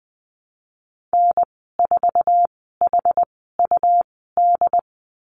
Morse code, heard as a keyed tone, about 20 wpm, spaced as for 14 wpm, 700 Hz. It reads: N4HUD
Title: 128 Lesson 21g - Number 6 - Call Signs 20wpm